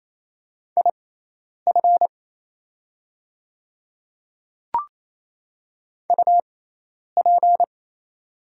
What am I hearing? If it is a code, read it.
Morse code, heard as a keyed tone, about 27 wpm, spaced as for 10 wpm, 690 Hz. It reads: IF UP
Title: 6x Character Spacing - Top 100 Words - 28wpm